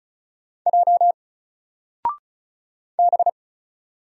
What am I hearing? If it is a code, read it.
Morse code, heard as a keyed tone, about 35 wpm, 700 Hz. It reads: J B
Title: Single Character (Letter:Number:Punctuation:Prosign) 35wpm